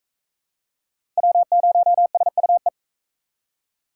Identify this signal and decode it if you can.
Morse code, heard as a keyed tone, about 36 wpm, 700 Hz. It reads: W0SUE